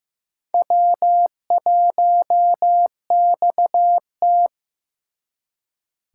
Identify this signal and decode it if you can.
Morse code, heard as a keyed tone, about 18 wpm, 700 Hz. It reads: W1XT
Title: Speed Racing - USA 1x2 Call Signs 15wpm